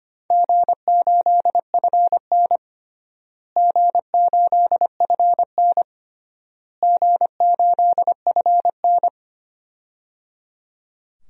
Morse code, heard as a keyed tone, about 25 wpm, 700 Hz. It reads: G8FN G8FN G8FN
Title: DXCC-100 Call Signs - 4 Characters - Repeated 3x - 25wpm